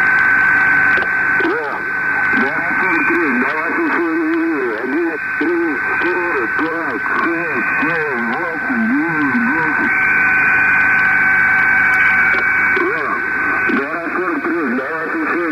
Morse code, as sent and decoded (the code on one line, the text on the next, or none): none